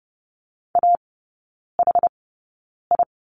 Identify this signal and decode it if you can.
Morse code, heard as a keyed tone, about 30 wpm, spaced as for 9 wpm, 700 Hz. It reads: AHI
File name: Lesson 18e - Letter m - Three Characters 30wpm